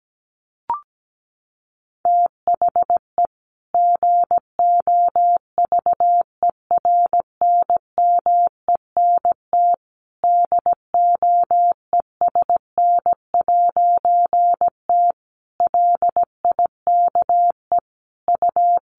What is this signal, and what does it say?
Morse code, heard as a keyed tone, about 17 wpm, 700 Hz. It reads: THE GOVERNMENT DOESN'T LIKE U
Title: A Sound Of Thunder 17wpm